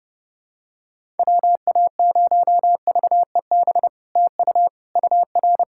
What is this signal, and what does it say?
Morse code, heard as a keyed tone, about 30 wpm, 700 Hz. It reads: WA0VEB TU UR
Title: SOTA QSO Practice 30wpm